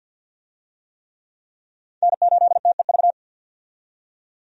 Morse code, heard as a keyed tone, about 52 wpm, 700 Hz. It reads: N8TEV
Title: USA Call Signs - 5 Characters - 50wpm